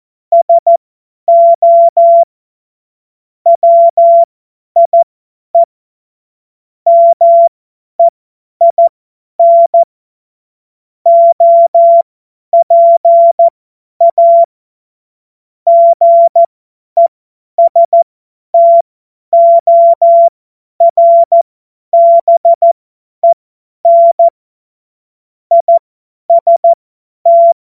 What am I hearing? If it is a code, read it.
Morse code, heard as a keyed tone, about 14 wpm, spaced as for 10 wpm, 680 Hz. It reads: SO WIE MEIN OPA GESTORBEN IST
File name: Morsekurs / Witze / Witz 26. Oktober 2021, 50 F70